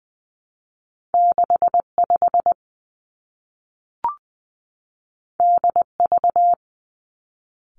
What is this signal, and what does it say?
Morse code, heard as a keyed tone, about 20 wpm, 700 Hz. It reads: 6 5 D V
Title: Two Challenging Letter-Number Combinations 20wpm